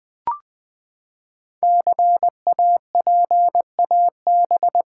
Morse code, heard as a keyed tone, about 20 wpm, 700 Hz. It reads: CAPAB
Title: Build A Word - 5 Letters At A Time - 10-Letter Words - 20wpm